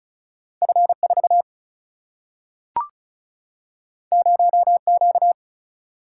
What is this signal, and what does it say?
Morse code, heard as a keyed tone, about 36 wpm, 700 Hz. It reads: F4 0Q
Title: Single-Letter Single-Number 35wpm